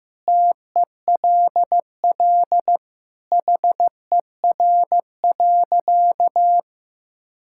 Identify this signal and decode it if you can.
Morse code, heard as a keyed tone, about 16 wpm, 700 Hz. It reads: TELL HER.
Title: Sentences from Top 1000 Words - Review Part 5 - 15wpm